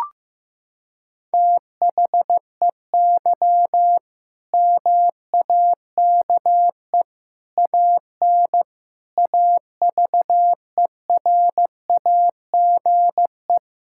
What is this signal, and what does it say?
Morse code, heard as a keyed tone, about 15 wpm, 700 Hz. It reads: THEY MAKE AN AVERAGE
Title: Sentences from Top 1000 Words - Review Part 6 - 15wpm